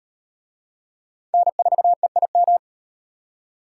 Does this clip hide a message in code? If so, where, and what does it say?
Yes: Morse code from 1.3 to 2.6 s, heard as a keyed tone, about 35 wpm, 700 Hz. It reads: N4EIM